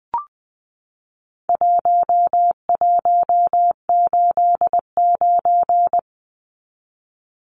Morse code, heard as a keyed tone, about 20 wpm, 700 Hz. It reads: 1189
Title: Numbers - 4-Digit 20wpm